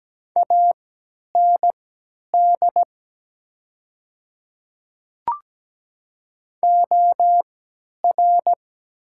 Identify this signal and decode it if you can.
Morse code, heard as a keyed tone, about 17 wpm, spaced as for 10 wpm, 700 Hz. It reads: AND OR
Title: 3x Character Spacing - Top 100 Words - 17wpm